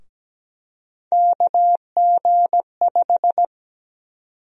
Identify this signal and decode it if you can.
Morse code, heard as a keyed tone, about 17 wpm, 700 Hz. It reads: KG5
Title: USA Call Sign Prefixes - Repeated 2x - 17wpm